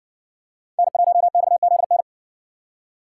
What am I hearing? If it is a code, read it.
Morse code, heard as a keyed tone, about 60 wpm, 700 Hz. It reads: N1XZN